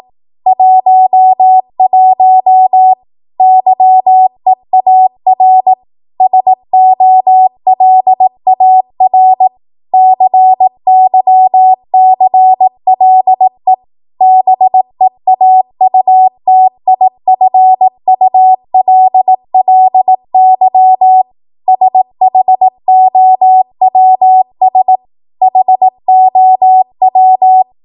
Morse code, heard as a keyed tone, about 18 wpm, 750 Hz. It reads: 11 YEAR SOLAR CYCLE BEAUTIFULLY SHOWS HOW
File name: Morse / 18WPM